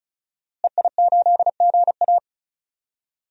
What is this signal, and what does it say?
Morse code, heard as a keyed tone, about 34 wpm, 700 Hz. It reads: EI8GA
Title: DXCC-100 Call Signs - 5 Characters - Repeated 3x - 35wpm